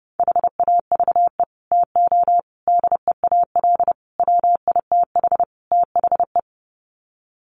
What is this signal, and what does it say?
Morse code, heard as a keyed tone, about 30 wpm, 700 Hz. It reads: HAVE TO DEAL WITH THE
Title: Sets of 5 Words Encore2 30wpm